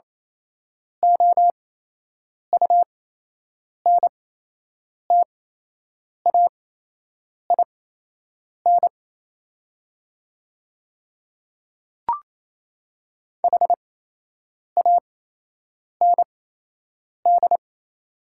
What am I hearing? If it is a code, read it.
Morse code, heard as a keyed tone, about 28 wpm, spaced as for 8 wpm, 700 Hz. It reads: OUNTAIN HAND
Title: 8x Character Spacing - Top 500 Words - 28wpm